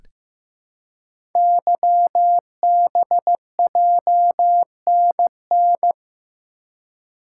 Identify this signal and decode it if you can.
Morse code, heard as a keyed tone, about 15 wpm, 700 Hz. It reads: YBJNN
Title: Code Groups - 5 Letters - 15wpm